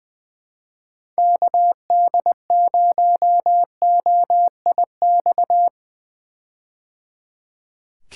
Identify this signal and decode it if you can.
Morse code, heard as a keyed tone, about 21 wpm, 700 Hz. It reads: KD0OIX
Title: K1USN SST QSO Practice - 20wpm